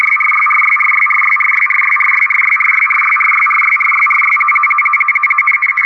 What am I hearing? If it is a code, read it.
Two-tone teleprinter signal, about 850 Hz shift.
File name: europe / rus / rus-4260-arkhangelsk-meteo-rtty-50 bd-(1995)-(rb)-(dat-917-14-xc)